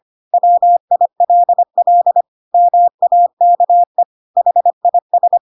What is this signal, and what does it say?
Morse code, heard as a keyed tone, about 25 wpm, 700 Hz. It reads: WILL MAKE HIS
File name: Pinocchio / Pinocchio ch3 25 25 ik1hge